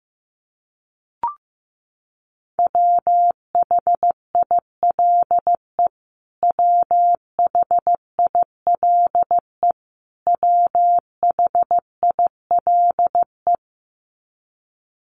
Morse code, heard as a keyed tone, about 14 wpm, 700 Hz. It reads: WHILE WHILE WHILE
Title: Top 500 Words Repeated 15wpm